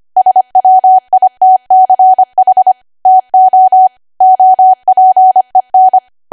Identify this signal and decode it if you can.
Morse code, heard as a keyed tone, about 25 wpm, 750 Hz. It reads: SWITCH TO OPEN
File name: cwa / W1AW-Level3 / 25WPM